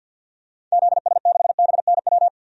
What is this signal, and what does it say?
Morse code, heard as a keyed tone, about 51 wpm, 700 Hz. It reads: ZS6BNW
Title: DXCC-100 Call Signs - 6 Characters - 50wpm